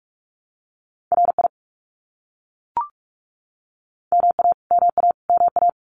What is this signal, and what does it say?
Morse code, heard as a keyed tone, about 44 wpm, 700 Hz. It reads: RI MA MA MA
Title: US States Repeated 3x 45wpm